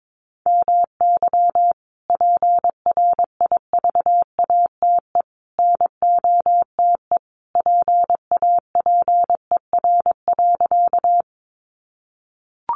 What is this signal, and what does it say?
Morse code, heard as a keyed tone, about 22 wpm, 700 Hz. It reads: MY PRIVATE NOTE PAPER.
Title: Sentences from Top 1000 Words 22wpm